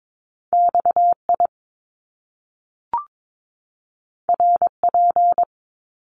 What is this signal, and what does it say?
Morse code, heard as a keyed tone, about 22 wpm, 700 Hz. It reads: XI RP
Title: Top Two Letter Combinations in Words 22wpm